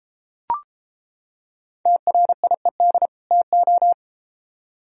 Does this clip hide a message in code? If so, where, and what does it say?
Yes: Morse code from 1.9 to 3.9 s, heard as a keyed tone, about 33 wpm, 700 Hz. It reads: TRIED TO